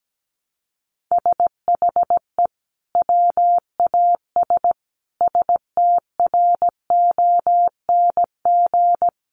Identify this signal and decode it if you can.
Morse code, heard as a keyed tone, about 17 wpm, 700 Hz. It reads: SHE WAS STRONG
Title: Sentences from Top 1200 Words 17wpm